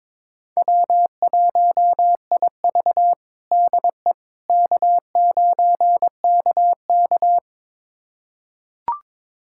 Morse code, heard as a keyed tone, about 22 wpm, 700 Hz. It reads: W1IV DE K9KK
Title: USA Callsign DE Callsign - 4 Characters - 22wpm